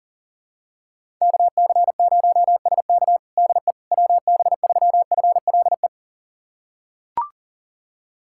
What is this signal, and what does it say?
Morse code, heard as a keyed tone, about 40 wpm, 700 Hz. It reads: KC0SK DE WB3FLE
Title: USA Callsign DE Callsign 40wpm